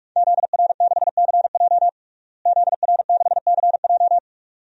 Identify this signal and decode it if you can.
Morse code, heard as a keyed tone, about 42 wpm, 700 Hz. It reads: ZR6CJ ZR6CJ
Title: DXCC-100 Call Signs - 5 Characters - Repeated 2x - 45wpm